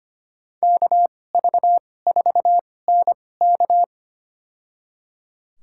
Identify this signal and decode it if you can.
Morse code, heard as a keyed tone, about 25 wpm, spaced as for 18 wpm, 700 Hz. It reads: KV4NK
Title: USA Call Signs - 2x Character Spacing - 25wpm